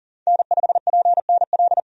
Morse code, heard as a keyed tone, about 40 wpm, 700 Hz. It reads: N5PNL
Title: USA Call Signs - 5 Characters - 40wpm